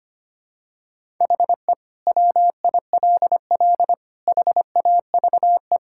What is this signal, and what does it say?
Morse code, heard as a keyed tone, about 25 wpm, 700 Hz. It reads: HE WILL HAVE
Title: Sentences from Top 1000 Words - Review Part 6 - 25wpm